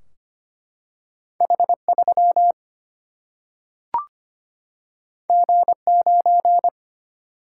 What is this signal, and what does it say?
Morse code, heard as a keyed tone, about 25 wpm, 700 Hz. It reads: H3 G9
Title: Single-Letter Single-Number 25wpm